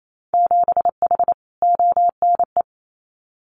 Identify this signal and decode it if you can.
Morse code, heard as a keyed tone, about 28 wpm, 700 Hz. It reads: 7H ONE
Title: Winter Field Day Exchanges Encore 28wpm